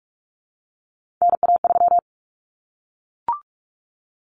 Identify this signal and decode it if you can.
Morse code, heard as a keyed tone, about 44 wpm, 700 Hz. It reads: NA3
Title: Common Cut Numbers - 3-Digit 45wpm